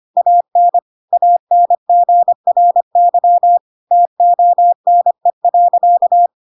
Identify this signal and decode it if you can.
Morse code, heard as a keyed tone, about 25 wpm, 700 Hz. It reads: AN ANGRY TONE.
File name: Pinocchio / Pinocchio ch7 25 25 ik1hge